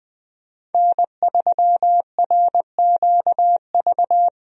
Morse code, heard as a keyed tone, about 20 wpm, 700 Hz. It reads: N3RQV
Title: USA Call Signs - 5 Characters - Repeated 3x - 20wpm